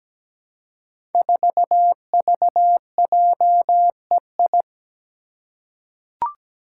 Morse code, heard as a keyed tone, about 17 wpm, 700 Hz. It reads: 4VJEI
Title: Five Challenging Letter-Number Combinations 17wpm